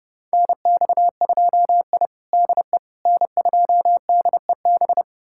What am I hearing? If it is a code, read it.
Morse code, heard as a keyed tone, about 30 wpm, 700 Hz. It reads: NX2I DE N2DEB
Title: USA Callsign DE Callsign 30wpm